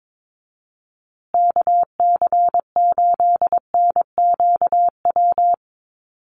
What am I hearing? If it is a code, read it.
Morse code, heard as a keyed tone, about 22 wpm, 690 Hz. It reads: KC8NQW
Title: USA Call Signs Repeated 3x 22wpm